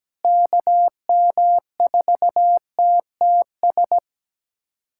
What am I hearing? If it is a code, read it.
Morse code, heard as a keyed tone, about 17 wpm, 700 Hz. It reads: KM4TTS
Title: USA Callsign DE Callsign - 6 Characters - 17wpm